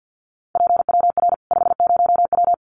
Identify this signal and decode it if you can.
Morse code, heard as a keyed tone, about 50 wpm, 700 Hz. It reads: PWR 50W
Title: Common QSO Phrases 50wpm